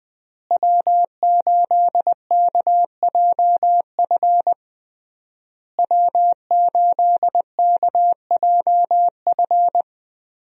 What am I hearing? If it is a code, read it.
Morse code, heard as a keyed tone, about 21 wpm, 700 Hz. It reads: W8KJF W8KJF
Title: USA Call Signs - 5 Characters - Repeated 3x - 20wpm